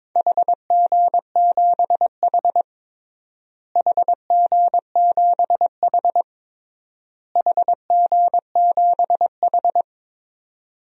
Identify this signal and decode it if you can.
Morse code, heard as a keyed tone, about 22 wpm, 700 Hz. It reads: HG7H HG7H HG7H
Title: DXCC-100 Call Signs - 4 Characters - Repeated 3x - 22wpm